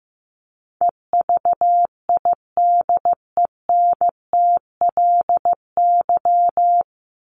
Morse code, heard as a keyed tone, about 15 wpm, 700 Hz. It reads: EVIDENTLY
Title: LY Suffix - Top 100 Words with Base Words 15wpm